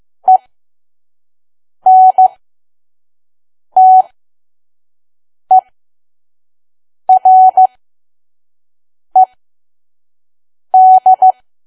Morse code, heard as a keyed tone, about 15 wpm, spaced as for 5 wpm, 750 Hz. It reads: ENTERED
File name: Archive / 5 WPM files / PM